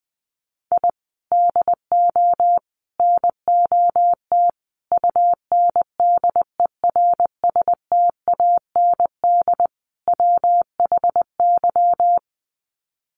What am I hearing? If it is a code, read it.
Morse code, heard as a keyed tone, about 20 wpm, 700 Hz. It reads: I DO NOT UNDERSTAND WHY
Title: Sets of 5 Words Encore 20wpm